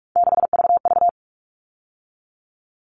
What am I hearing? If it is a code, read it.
Morse code, heard as a keyed tone, about 44 wpm, 700 Hz. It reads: 6VV
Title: All Cut Numbers - 3-Digit 45wpm